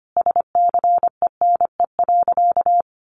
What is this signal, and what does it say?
Morse code, heard as a keyed tone, about 25 wpm, 700 Hz. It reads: SCENE.
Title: Sentences from Top 1100 Words 25wpm